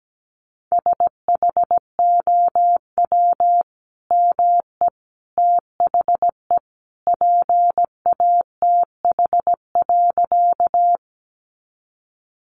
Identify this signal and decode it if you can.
Morse code, heard as a keyed tone, about 17 wpm, 700 Hz. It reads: SHOW ME THE PATH.
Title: Sentences from Top 1400 Words 17wpm